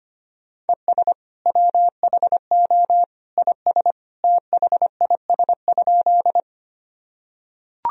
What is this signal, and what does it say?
Morse code, heard as a keyed tone, about 25 wpm, 700 Hz. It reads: ES WHO IS THIS?
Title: Sentences from Top 100 Words and Abbreviations 25wpm